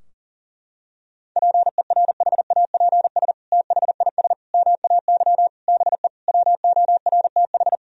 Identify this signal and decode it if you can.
Morse code, heard as a keyed tone, about 37 wpm, 700 Hz. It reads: PERHAPS THIS MAY BE WORTH
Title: Sentences from Top 1000 Words - Review Part 2 - 40wpm